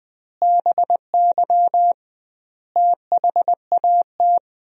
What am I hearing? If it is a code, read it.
Morse code, heard as a keyed tone, about 21 wpm, 700 Hz. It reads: BY THAT
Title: 2x Word-Spacing - Sentences from Top 400 Words - 20wpm